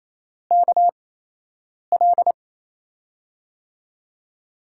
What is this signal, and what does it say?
Morse code, heard as a keyed tone, about 28 wpm, spaced as for 8 wpm, 700 Hz. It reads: KL
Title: USA Call Signs - 6 Characters - 8x Character Spacing - 28wpm